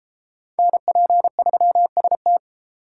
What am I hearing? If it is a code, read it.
Morse code, heard as a keyed tone, about 31 wpm, 700 Hz. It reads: NP3ST